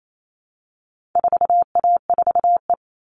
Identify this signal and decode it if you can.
Morse code, heard as a keyed tone, about 28 wpm, 700 Hz. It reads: 4A4E
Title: All Cut Numbers - 4-Digit 28wpm